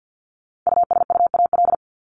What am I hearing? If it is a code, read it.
Morse code, heard as a keyed tone, about 50 wpm, 700 Hz. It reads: USUAL
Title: Top 500 Words Repeated 50wpm